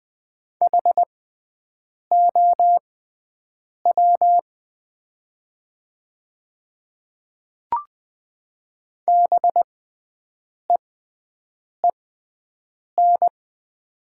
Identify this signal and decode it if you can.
Morse code, heard as a keyed tone, about 20 wpm, spaced as for 7 wpm, 700 Hz. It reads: HOW BEEN
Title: 6x Character Spacing - Top 100 Words - 20wpm